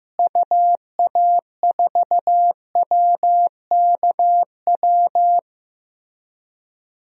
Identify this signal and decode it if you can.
Morse code, heard as a keyed tone, about 15 wpm, 700 Hz. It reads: UA4WKW
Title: DXCC-100 Call Signs - 6 Characters - Repeated 3x - 15wpm